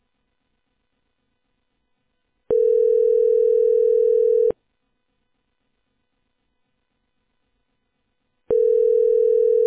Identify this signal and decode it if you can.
Ringback tone (North America), 440 + 480 Hz.